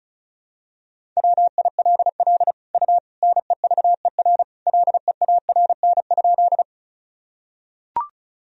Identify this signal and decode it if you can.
Morse code, heard as a keyed tone, about 34 wpm, 700 Hz. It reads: WILL U NEVER LEARN?
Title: Sentences from Top 300 Words and Abbreviations 35wpm